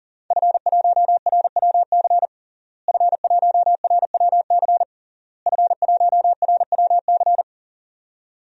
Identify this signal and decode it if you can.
Morse code, heard as a keyed tone, about 40 wpm, 700 Hz. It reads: F1RWC F1RWC F1RWC
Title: DXCC-100 Call Signs - 5 Characters - Repeated 3x - 40wpm